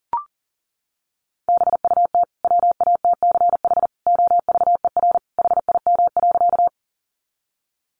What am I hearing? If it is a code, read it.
Morse code, heard as a keyed tone, about 40 wpm, 700 Hz. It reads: BUT WATCH OVER HIM.
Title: Sentences from Top 300 Words 40wpm